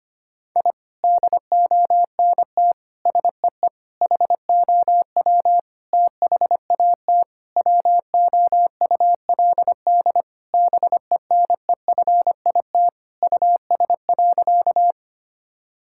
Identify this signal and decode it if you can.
Morse code, heard as a keyed tone, about 25 wpm, 700 Hz. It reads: I DONT SEE HOW THAT WOULD BENEFIT US.
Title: Sentences from Top 1000 Words - Review Part 6 - 25wpm